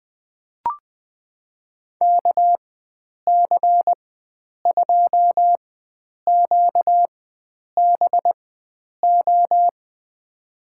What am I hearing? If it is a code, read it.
Morse code, heard as a keyed tone, about 20 wpm, spaced as for 9 wpm, 700 Hz. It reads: KC2QBO
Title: USA Call Signs - 6 Characters - 4x Character Spacing - 20wpm